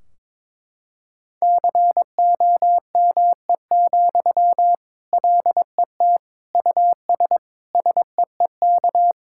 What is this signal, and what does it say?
Morse code, heard as a keyed tone, about 23 wpm, 700 Hz. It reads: COME, LET US SEEK
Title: Sentences from Top 1600 Words 22wpm